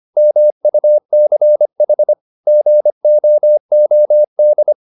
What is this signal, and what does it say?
Morse code, heard as a keyed tone, about 25 wpm, 600 Hz. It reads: MUCH GOOD